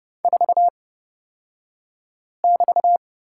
Morse code, heard as a keyed tone, about 29 wpm, spaced as for 5 wpm, 700 Hz. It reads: 4<BT>
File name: Lesson 36d - BT - Two Characters 30wpm